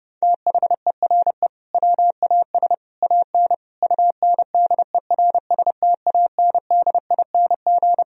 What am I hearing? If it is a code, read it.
Morse code, heard as a keyed tone, about 30 wpm, 700 Hz. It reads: THERE WAS AN UNDERSTANDING
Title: Sentences from Top 1000 Words - Review Part 3 - 30wpm